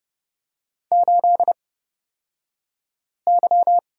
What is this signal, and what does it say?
Morse code, heard as a keyed tone, about 29 wpm, spaced as for 5 wpm, 700 Hz. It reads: 8Y
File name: Lesson 38d - Letter q - Two Characters 30wpm